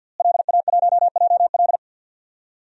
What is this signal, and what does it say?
Morse code, heard as a keyed tone, about 51 wpm, 700 Hz. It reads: LA1JL
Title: DXCC-100 Call Signs - 5 Characters - Repeated 2x - 50wpm